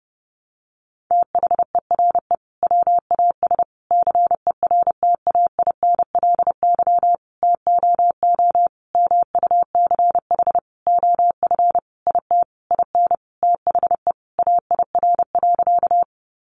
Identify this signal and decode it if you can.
Morse code, heard as a keyed tone, about 30 wpm, 700 Hz. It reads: THERE WAS CERTAINLY TOO MUCH OF IT IN THE AIR.
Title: Sentences from Top 900 Words 30wpm